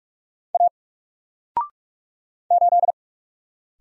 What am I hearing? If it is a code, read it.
Morse code, heard as a keyed tone, about 44 wpm, 700 Hz. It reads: A 8
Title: Single Letter-Number 45wpm